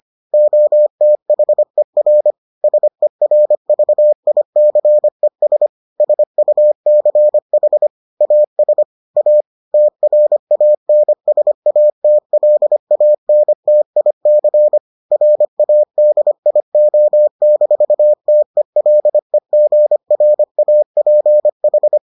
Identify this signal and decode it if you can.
Morse code, heard as a keyed tone, about 25 wpm, 600 Hz. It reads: OTHER SERVICES SUCH AS A TRANSATLANTIC RADIO-TELEGRAPH